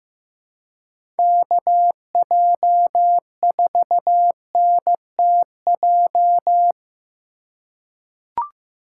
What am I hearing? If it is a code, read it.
Morse code, heard as a keyed tone, about 15 wpm, 700 Hz. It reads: KJ4NTJ
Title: USA Call Signs - 6 Characters - 2x Character Spacing - 15wpm